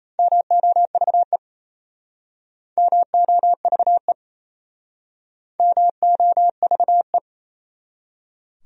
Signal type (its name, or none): Morse code